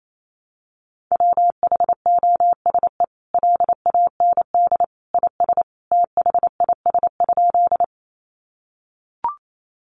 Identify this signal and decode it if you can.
Morse code, heard as a keyed tone, about 28 wpm, 700 Hz. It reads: WHOSE LAND IS THIS?
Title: Sentences from Top 1200 Words 28wpm